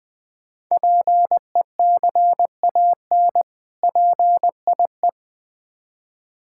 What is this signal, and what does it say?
Morse code, heard as a keyed tone, about 21 wpm, 700 Hz. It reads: PECAN PIE
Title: Difficult Words with Hints - 20wpm